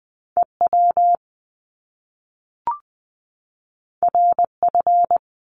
Morse code, heard as a keyed tone, about 20 wpm, 700 Hz. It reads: EW RF
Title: Top Two Letter Combinations in Words 20wpm